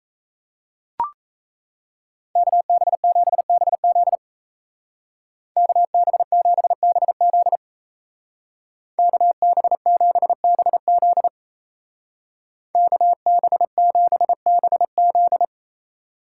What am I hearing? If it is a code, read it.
Morse code, heard as a keyed tone, about 33 wpm, 700 Hz. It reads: KB7BZ KB7BZ KB7BZ KB7BZ
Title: Speed Racing - USA 2x2 Call Signs 28wpm